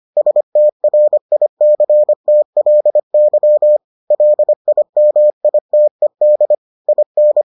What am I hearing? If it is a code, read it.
Morse code, heard as a keyed tone, about 25 wpm, 600 Hz. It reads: STRICTLY LIMITED IN